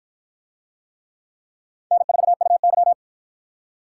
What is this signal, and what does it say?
Morse code, heard as a keyed tone, about 52 wpm, 700 Hz. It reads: N4UY